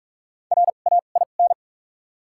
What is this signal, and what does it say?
Morse code, heard as a keyed tone, about 48 wpm, spaced as for 33 wpm, 700 Hz. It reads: RAIN
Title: 2x Character Spacing - Top 300 Words - 45wpm